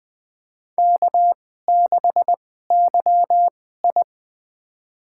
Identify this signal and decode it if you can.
Morse code, heard as a keyed tone, about 20 wpm, spaced as for 15 wpm, 700 Hz. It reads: K6YI
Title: USA Call Signs - 4 Characters - 2x Character Spacing - 20wpm